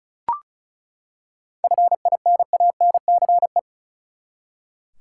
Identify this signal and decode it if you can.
Morse code, heard as a keyed tone, about 34 wpm, 700 Hz. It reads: FINANCE
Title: Seven Letter Words - 35wpm